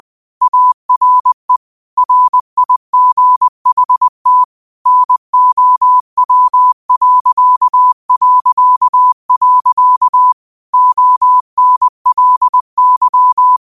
Morse code, heard as a keyed tone, about 20 wpm, 1000 Hz. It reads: ARE RIGHT NOW... ONLY
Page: Quotes for Thu, 14 Aug 2025 in Morse Code at 20 words per minute.